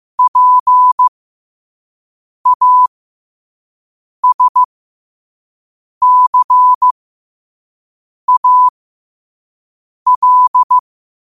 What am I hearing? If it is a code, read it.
Morse code, heard as a keyed tone, about 15 wpm, spaced as for 5 wpm, 1000 Hz. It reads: PASCAL
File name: qotdfiles / 5wpm